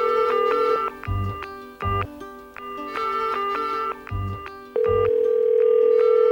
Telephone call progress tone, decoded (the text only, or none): ringback tone